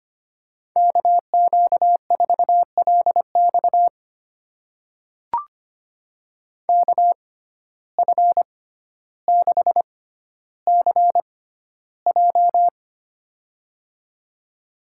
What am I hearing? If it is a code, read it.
Morse code, heard as a keyed tone, about 25 wpm, spaced as for 13 wpm, 700 Hz. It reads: KQ4LX KF6CJ
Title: USA Call Signs - 5 Characters - 6x Character Spacing - 25wpm